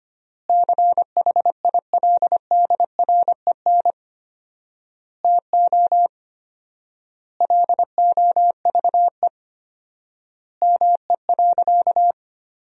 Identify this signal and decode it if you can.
Morse code, heard as a keyed tone, about 25 wpm, 700 Hz. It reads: CHILDREN TO LOVE ME.
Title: 4x Word-Spacing - Sentences from Top 400 Words - 25wpm